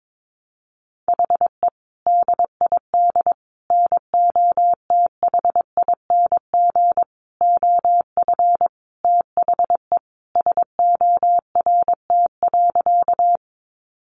Morse code, heard as a keyed tone, about 22 wpm, 700 Hz. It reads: HE DID NOTHING OF THE SORT.
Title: Sentences from Top 1300 Words 22wpm